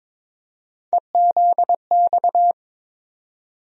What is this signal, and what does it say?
Morse code, heard as a keyed tone, about 22 wpm, 700 Hz. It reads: EZX
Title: USA Call Sign Postfixes - 3 Letters 22wpm